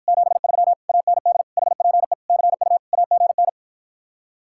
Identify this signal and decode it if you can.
Morse code, heard as a keyed tone, about 54 wpm, 700 Hz. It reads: 73 AND HPE CU AGN